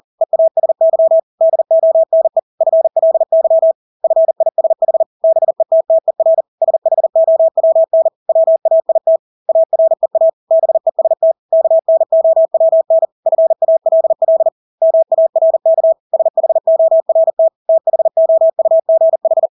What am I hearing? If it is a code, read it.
Morse code, heard as a keyed tone, about 40 wpm, 650 Hz. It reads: EASY DONE FLY FISH BETTER SHOWN WAIT AREA BEST KNOWN FALL MARK SHORT THOUGH
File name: Engelske ord 40-40 wpm